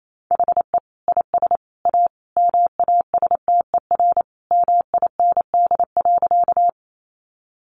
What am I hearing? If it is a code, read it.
Morse code, heard as a keyed tone, about 28 wpm, 700 Hz. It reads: HE IS A MASTER MIND.
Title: Sentences from Top 1300 Words 28wpm